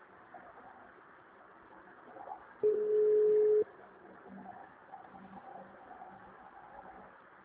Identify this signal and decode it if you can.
Ringback tone (Europe, 425 Hz).
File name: out- -2029- - - .14521